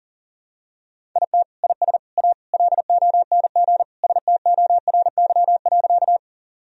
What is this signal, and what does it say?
Morse code, heard as a keyed tone, about 40 wpm, 700 Hz. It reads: IT IS A LONG STORY.
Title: Sentences from Top 1000 Words - Review Part 1 - 40wpm